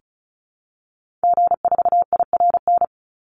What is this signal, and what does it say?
Morse code, heard as a keyed tone, about 34 wpm, 700 Hz. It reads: G4IRN
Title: Call Signs - 35wpm